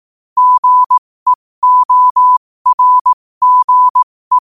Morse code, heard as a keyed tone, about 18 wpm, spaced as for 15 wpm, 1000 Hz. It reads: GEORGE
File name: qotdfiles / 15wpm